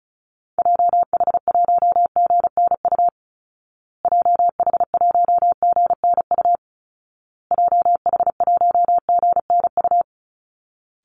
Morse code, heard as a keyed tone, about 34 wpm, 700 Hz. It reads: JH1GNU JH1GNU JH1GNU
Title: Call Signs Repeated 2x - 35wpm